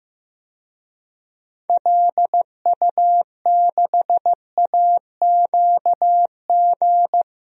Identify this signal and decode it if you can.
Morse code, heard as a keyed tone, about 15 wpm, 700 Hz. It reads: LU6AQG
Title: DXCC-100 Call Signs - 6 Characters - 15wpm